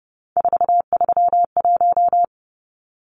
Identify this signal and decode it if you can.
Morse code, heard as a keyed tone, about 30 wpm, 700 Hz. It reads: 431
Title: Numbers - 3-Digit 30wpm